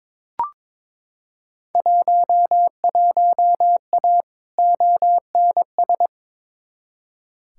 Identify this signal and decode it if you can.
Morse code, heard as a keyed tone, about 21 wpm, 700 Hz. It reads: 11A ONS
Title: Field Day Exchanges Encore - 22wpm